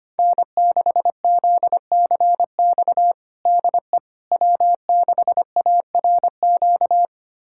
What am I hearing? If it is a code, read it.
Morse code, heard as a keyed tone, about 25 wpm, 700 Hz. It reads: N6ZCX DE W6ARQ
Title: USA Callsign DE Callsign - 5 Characters - 25wpm